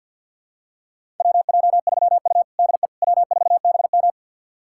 Morse code, heard as a keyed tone, about 51 wpm, 700 Hz. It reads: WJ3U DE W4BM
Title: USA Callsign DE Callsign - 4 Characters - 50wpm